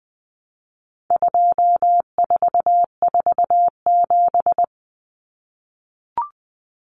Morse code, heard as a keyed tone, about 20 wpm, 700 Hz. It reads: 2447
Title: Numbers - 4-Digit 20wpm